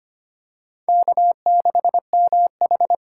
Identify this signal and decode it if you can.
Morse code, heard as a keyed tone, about 25 wpm, 700 Hz. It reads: K6MH
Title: USA Call Signs - 4 Characters - 4x Character Spacing - 25wpm